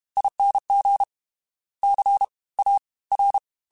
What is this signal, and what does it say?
Morse code, heard as a keyed tone, about 32 wpm, 800 Hz. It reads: ING C A R
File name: Special INGs / Special INGs-mp3s / Fast ing @30